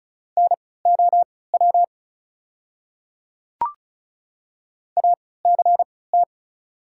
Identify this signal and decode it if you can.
Morse code, heard as a keyed tone, about 36 wpm, spaced as for 20 wpm, 700 Hz. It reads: NOW ACT
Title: 3x Character Spacing - Top 200 Words - 35wpm